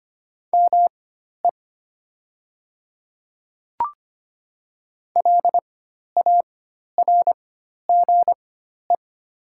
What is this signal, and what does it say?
Morse code, heard as a keyed tone, about 25 wpm, spaced as for 12 wpm, 690 Hz. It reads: ME LARGE
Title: 4x Character Spacing - Top 200 Words - 25wpm